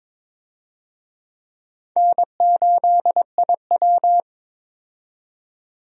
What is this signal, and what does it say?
Morse code, heard as a keyed tone, about 22 wpm, 700 Hz. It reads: N8IW
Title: USA Call Signs - 4 Characters - 22wpm